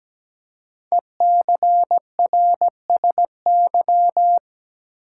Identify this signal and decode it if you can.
Morse code, heard as a keyed tone, about 17 wpm, 700 Hz. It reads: ECRSY
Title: Code Groups - 5 Letters - 17wpm